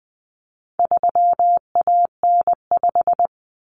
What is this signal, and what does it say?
Morse code, heard as a keyed tone, about 20 wpm, 700 Hz. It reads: 3AN5
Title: Common Cut Numbers - 4-Digit 20wpm